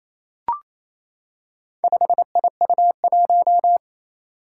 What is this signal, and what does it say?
Morse code, heard as a keyed tone, about 27 wpm, 700 Hz. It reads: 5IU1
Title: Four Challenging Letter-Number Combinations 28wpm